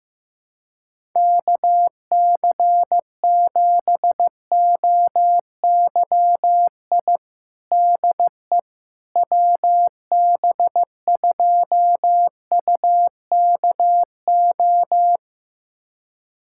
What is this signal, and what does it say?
Morse code, heard as a keyed tone, about 15 wpm, 700 Hz. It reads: KC7OYI DE WB2UKO
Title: USA Callsign DE Callsign - 6 Characters - 15wpm